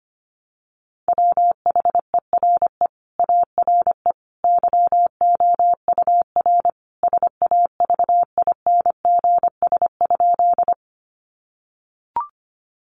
Morse code, heard as a keyed tone, about 25 wpm, 700 Hz. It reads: WHERE ARE YOUR SAVINGS?
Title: Sentences from Top 1200 Words 25wpm